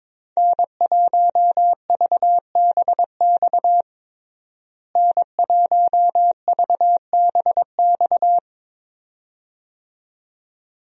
Morse code, heard as a keyed tone, about 22 wpm, 700 Hz. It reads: N1VBX N1VBX
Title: USA Call Signs - 5 Characters - Repeated 2x - 22wpm